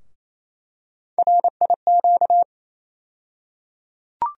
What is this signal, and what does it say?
Morse code, heard as a keyed tone, about 28 wpm, 700 Hz. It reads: RIQ
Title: USA Call Sign Postfixes - 3 Letters 28wpm